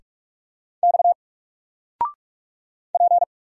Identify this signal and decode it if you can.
Morse code, heard as a keyed tone, about 44 wpm, 700 Hz. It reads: X P
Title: Single Letter-Number 45wpm